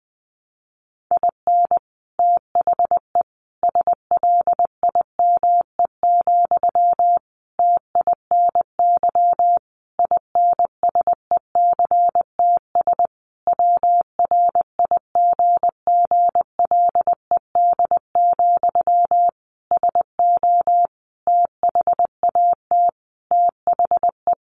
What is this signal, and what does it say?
Morse code, heard as a keyed tone, about 19 wpm, 700 Hz. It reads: IN THE SLIME, TINY INSECTS WRIGGLED, SO THAT THE
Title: A Sound Of Thunder 20wpm